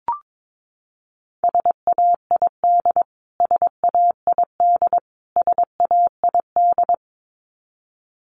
Morse code, heard as a keyed tone, about 22 wpm, 700 Hz. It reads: SAID SAID SAID
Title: Top 200 Words Repeated 22wpm